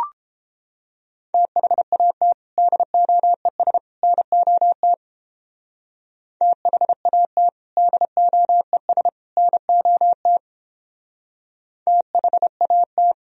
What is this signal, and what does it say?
Morse code, heard as a keyed tone, about 29 wpm, 700 Hz. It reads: THAT DOES NOT THAT DOES NOT THAT